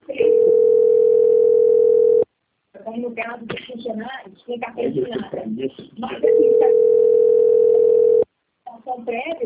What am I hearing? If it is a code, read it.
Ringback tone (North America), 440 + 480 Hz, with speech.
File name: out- - - - - .23859